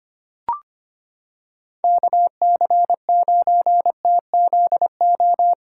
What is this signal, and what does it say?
Morse code, heard as a keyed tone, about 25 wpm, 700 Hz. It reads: KC9TZO
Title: K1USN SST QSO Practice - 25wpm